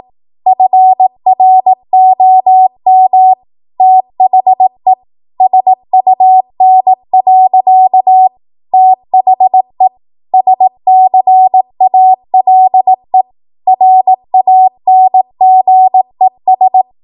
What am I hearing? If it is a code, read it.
Morse code, heard as a keyed tone, about 18 wpm, 750 Hz. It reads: FROM THE SUN. THE SCALE RANGES
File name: Morse / 18WPM